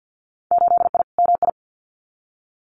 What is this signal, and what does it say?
Morse code, heard as a keyed tone, about 50 wpm, 700 Hz. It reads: 8I MI
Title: Winter Field Day Exchanges Encore 50wpm